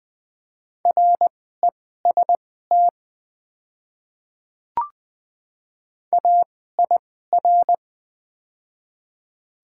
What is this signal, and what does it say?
Morse code, heard as a keyed tone, about 20 wpm, spaced as for 14 wpm, 700 Hz. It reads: REST AIR
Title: 2x Character Spacing - Top 500 Words - 20wpm